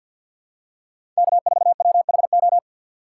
Morse code, heard as a keyed tone, about 51 wpm, 700 Hz. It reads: K4WHO
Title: USA Call Signs - 5 Characters - 8x Character Spacing - 50wpm